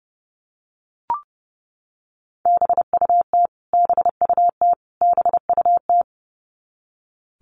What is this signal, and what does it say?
Morse code, heard as a keyed tone, about 30 wpm, 700 Hz. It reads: BUT BUT BUT
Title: Top 200 Words Repeated 30wpm